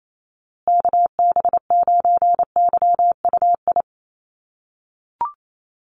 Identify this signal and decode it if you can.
Morse code, heard as a keyed tone, about 28 wpm, 700 Hz. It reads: KB9YUI
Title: USA Call Signs Repeated 3x 28wpm